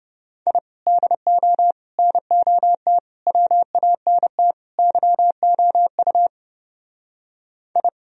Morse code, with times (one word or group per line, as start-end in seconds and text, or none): 0.5-0.6 I
0.9-1.7 DO
2.0-3.0 NOT
3.3-4.5 WANT
4.8-6.3 YOU
7.8-7.9 I